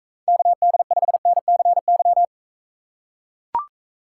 Morse code, heard as a keyed tone, about 42 wpm, 700 Hz. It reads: KD5NCY